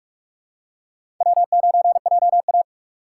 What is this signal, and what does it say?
Morse code, heard as a keyed tone, about 45 wpm, 700 Hz. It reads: W9JA
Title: USA Call Signs - 4 Characters - 4x Character Spacing - 45wpm